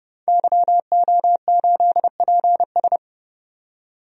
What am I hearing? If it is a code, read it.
Morse code, heard as a keyed tone, about 30 wpm, 700 Hz. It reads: YO8PS
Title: DXCC-100 Call Signs - 5 Characters - 30wpm